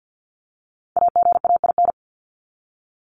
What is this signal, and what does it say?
Morse code, heard as a keyed tone, about 48 wpm, 700 Hz. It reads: AGAIN
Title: Top 200 Words 50wpm